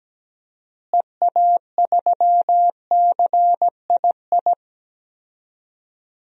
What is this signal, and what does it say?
Morse code, heard as a keyed tone, about 17 wpm, 700 Hz. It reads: EA3CII
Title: DXCC-100 Call Signs - 6 Characters - Repeated 2x - 17wpm